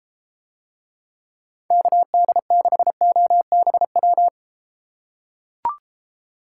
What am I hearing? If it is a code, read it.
Morse code, heard as a keyed tone, about 31 wpm, 700 Hz. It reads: KD6OBW